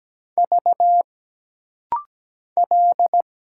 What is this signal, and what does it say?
Morse code, heard as a keyed tone, about 17 wpm, 700 Hz. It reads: V L
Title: Single Letters - Rapid-Fire 17wpm